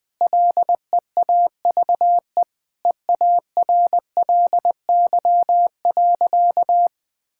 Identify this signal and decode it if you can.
Morse code, heard as a keyed tone, about 20 wpm, 700 Hz. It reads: LEAVE EARLY.
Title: Sentences with 3-Word Phrases - 20wpm